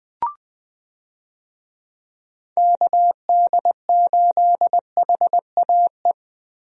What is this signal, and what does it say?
Morse code, heard as a keyed tone, about 21 wpm, 700 Hz. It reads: KD8HAE
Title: USA Call Signs - 6 Characters - Repeated 3x - 20wpm